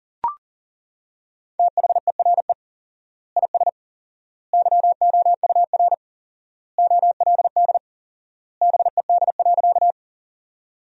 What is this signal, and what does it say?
Morse code, heard as a keyed tone, about 40 wpm, 700 Hz. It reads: THERE IS YOUR OLD BED.
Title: 4x Word-Spacing - Sentences from Top 500 Words - 40wpm